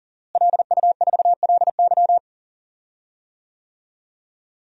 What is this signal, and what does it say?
Morse code, heard as a keyed tone, about 40 wpm, 700 Hz. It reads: LU4LY
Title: DXCC-100 Call Signs - 5 Characters - Repeated 3x - 40wpm